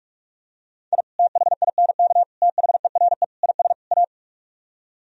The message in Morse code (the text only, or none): I THINK THERE IS A